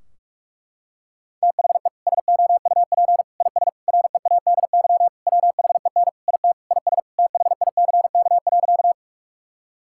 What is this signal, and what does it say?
Morse code, heard as a keyed tone, about 46 wpm, 700 Hz. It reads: THE SOUP IS READY WHEN IT IS THICK.
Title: Sentences from Top 1600 Words 45wpm